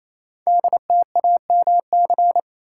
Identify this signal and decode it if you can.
Morse code, heard as a keyed tone, about 27 wpm, 700 Hz. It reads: DTAMC
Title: Code Groups - 5 Letters - 28wpm